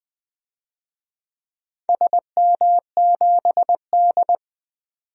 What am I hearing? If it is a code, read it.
Morse code, heard as a keyed tone, about 20 wpm, 700 Hz. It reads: SM7D
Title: DXCC-100 Call Signs - 4 Characters - 20wpm